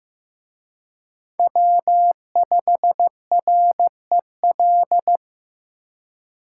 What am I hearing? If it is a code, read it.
Morse code, heard as a keyed tone, about 18 wpm, 700 Hz. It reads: W5REL
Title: Speed Racing - USA 1x3 Call Signs 15wpm